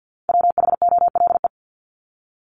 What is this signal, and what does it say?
Morse code, heard as a keyed tone, about 50 wpm, 700 Hz. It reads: WHOLE
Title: Top 500 Words 50wpm